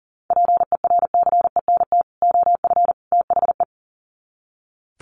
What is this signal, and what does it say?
Morse code, heard as a keyed tone, about 40 wpm, 700 Hz. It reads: PERCENT OF THE
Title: Sets of 3 Words 40wpm